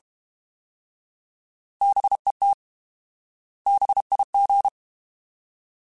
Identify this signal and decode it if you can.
Morse code, heard as a keyed tone, about 32 wpm, 800 Hz. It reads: BET BIG
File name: Alphabetical / Alphabetical mp3s / Bs @30